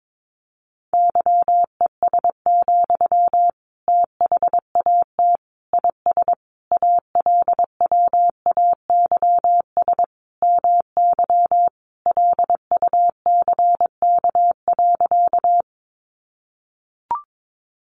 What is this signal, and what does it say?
Morse code, heard as a keyed tone, about 22 wpm, 700 Hz. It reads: YES, THAT IS ALWAYS MY LUCK.
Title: Sentences from Top 1500 Words 22wpm